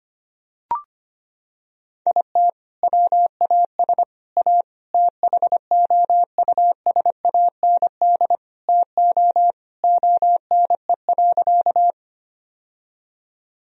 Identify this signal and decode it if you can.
Morse code, heard as a keyed tone, about 25 wpm, 700 Hz. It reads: IT WAS A THOUSAND TO ONE.
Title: Sentences from Top 1000 Words - Review Part 3 - 25wpm